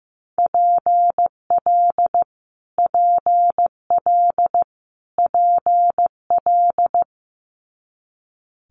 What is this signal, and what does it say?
Morse code, heard as a keyed tone, about 15 wpm, 700 Hz. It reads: PL PL PL
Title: Multi-letter Phonemes Repeated 3x 15wpm